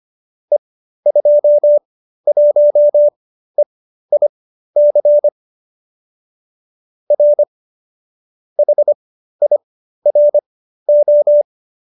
Morse code, heard as a keyed tone, about 25 wpm, spaced as for 13 wpm, 570 Hz. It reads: E21EIC R HIRO
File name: qso208b 13